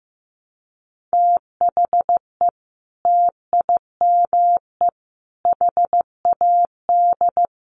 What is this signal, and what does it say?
Morse code, heard as a keyed tone, about 15 wpm, 700 Hz. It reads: THE TIME HAD
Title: Sentences from Top 100 Words 15wpm